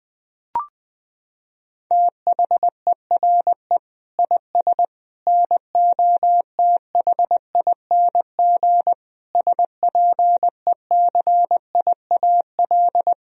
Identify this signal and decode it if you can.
Morse code, heard as a keyed tone, about 21 wpm, 700 Hz. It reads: THERE IS NOTHING SPECIAL
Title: Sentences from Top 1000 Words - Review Part 3 - 20wpm